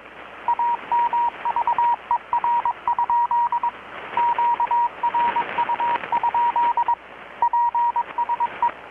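Morse code, heard as a keyed tone, about 21 wpm, 950 Hz, with radio noise. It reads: AMVER? QRU? PSE
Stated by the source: Morse from KPH's Night of Nights XX maritime radiotelegraphy event, received in Chiapas, Mexico about 3,800 km away on a Kenwood R-600 with a grounded 10 m long-wire antenna.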